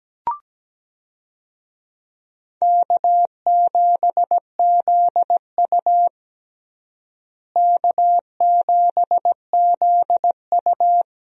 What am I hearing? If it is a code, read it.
Morse code, heard as a keyed tone, about 17 wpm, 700 Hz. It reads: K7ZU K7ZU
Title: USA Call Signs - 4 Characters - Repeated 2x - 17wpm